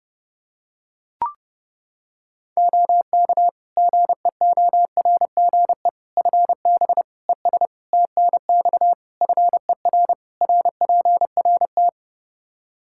Morse code, heard as a keyed tone, about 30 wpm, 700 Hz. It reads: OK GEORGE FB ES TNX FER RPRT
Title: K1USN SST QSO Practice 30wpm